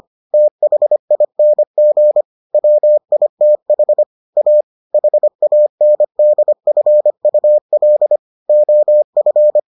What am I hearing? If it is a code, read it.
Morse code, heard as a keyed tone, about 25 wpm, 600 Hz. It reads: THING WITH A HANDFUL OF